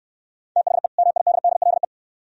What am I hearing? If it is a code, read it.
Morse code, heard as a keyed tone, about 69 wpm, 700 Hz. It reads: THE MIDDLE